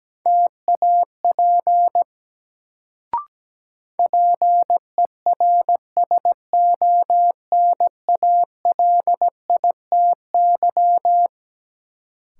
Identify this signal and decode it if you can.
Morse code, heard as a keyed tone, about 17 wpm, 700 Hz. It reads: TAP PERSONALITY
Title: Sentences from Top 1400 Words - New Word Drill Alt - 17wpm